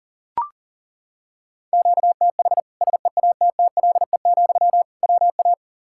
Morse code, heard as a keyed tone, about 40 wpm, 700 Hz. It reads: QTH SEATTLE, WA
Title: Practice QSO - Exchange 1 - Beginning - 40wpm